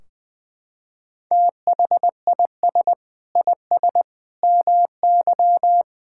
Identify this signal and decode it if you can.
Morse code, heard as a keyed tone, about 20 wpm, 700 Hz. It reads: THIS IS MY
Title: Sentences from Top 1600 Words 20wpm